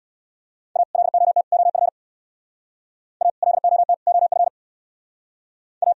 Morse code, heard as a keyed tone, about 64 wpm, 700 Hz. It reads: A LOT OF A LOT OF A